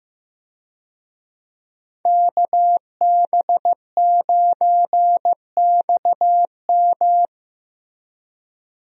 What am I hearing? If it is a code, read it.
Morse code, heard as a keyed tone, about 16 wpm, 700 Hz. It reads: KB9XM
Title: USA Call Signs - 5 Characters - Repeated 3x - 15wpm